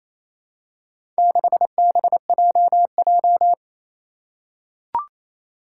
Morse code, heard as a keyed tone, about 28 wpm, 700 Hz. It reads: 6BJJ
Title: Four Challenging Letter-Number Combinations 28wpm